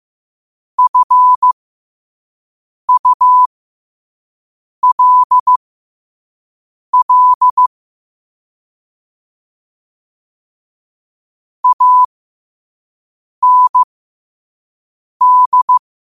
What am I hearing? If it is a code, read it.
Morse code, heard as a keyed tone, about 15 wpm, spaced as for 5 wpm, 1000 Hz. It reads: FULL AND
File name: qotdfiles / 5wpm